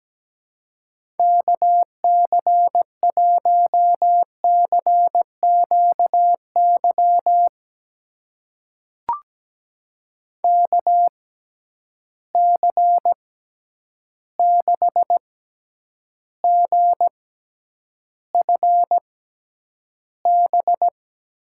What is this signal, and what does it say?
Morse code, heard as a keyed tone, about 17 wpm, spaced as for 9 wpm, 700 Hz. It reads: KC1CQY KC6GFB
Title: USA Call Signs - 6x Character Spacing - 17wpm